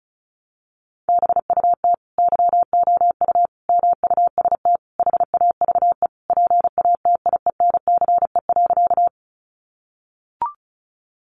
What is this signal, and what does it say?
Morse code, heard as a keyed tone, about 34 wpm, 700 Hz. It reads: BUT YOU MUST HAVE PATIENCE.
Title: Sentences from Top 1300 Words 35wpm